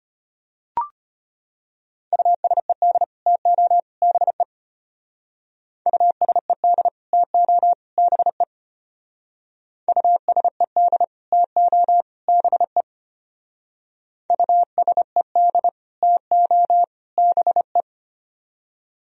Morse code, heard as a keyed tone, about 32 wpm, 700 Hz. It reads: USED TO BE USED TO BE USED TO BE USED TO BE